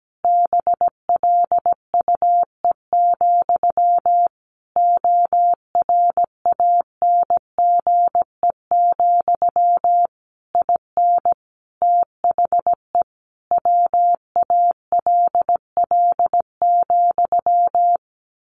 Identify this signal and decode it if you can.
Morse code, heard as a keyed tone, about 17 wpm, 700 Hz. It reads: BLUE, ORANGE, IN THE WALL,
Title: A Sound Of Thunder 17wpm